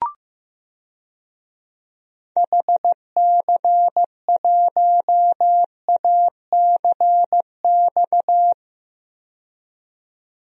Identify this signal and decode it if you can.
Morse code, heard as a keyed tone, about 15 wpm, 700 Hz. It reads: HC1ACX
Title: DXCC-100 Call Signs - 6 Characters - 15wpm